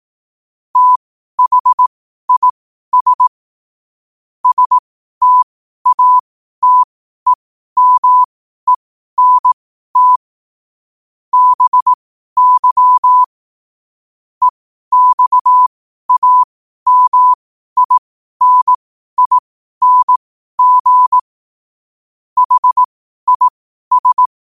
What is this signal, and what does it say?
Morse code, heard as a keyed tone, about 18 wpm, spaced as for 12 wpm, 1000 Hz. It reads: THIS STATEMENT BY EXAMINING HIS
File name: qotdfiles / 12wpm